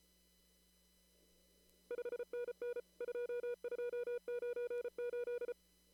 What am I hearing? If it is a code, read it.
Morse code, heard as a keyed tone, about 34 wpm, 470 Hz. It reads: 5NN 2298
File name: CW / kt5j wpx cw-op-k5pi